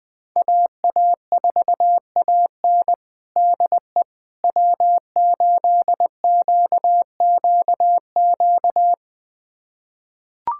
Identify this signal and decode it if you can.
Morse code, heard as a keyed tone, about 20 wpm, 700 Hz. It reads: AA4AN DE W8QQQ
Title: USA Callsign DE Callsign 20wpm